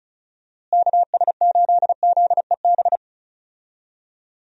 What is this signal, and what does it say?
Morse code, heard as a keyed tone, about 36 wpm, 700 Hz. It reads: KS8ZEB
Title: USA Call Signs - 6 Characters - Repeated 3x - 35wpm